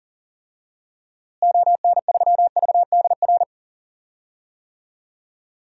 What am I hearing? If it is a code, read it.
Morse code, heard as a keyed tone, about 38 wpm, 700 Hz. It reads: ON3VDR
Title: DXCC-100 Call Signs - 6 Characters - Repeated 2x - 40wpm